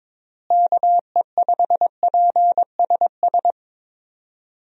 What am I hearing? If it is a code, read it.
Morse code, heard as a keyed tone, about 22 wpm, 700 Hz. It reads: KE5PSS
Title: USA Call Signs - 6 Characters - Repeated 3x - 22wpm